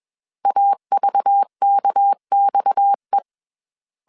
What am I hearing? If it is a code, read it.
Morse code, heard as a keyed tone, about 21 wpm, 800 Hz. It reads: AVKXE